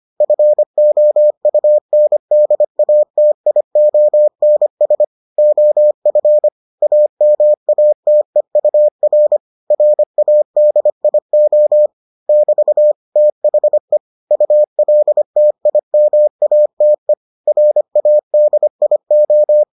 Morse code, heard as a keyed tone, about 25 wpm, 600 Hz. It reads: FOUNDATIONS OF AMATEUR RADIO <BT> THE ULTIMATE RADIO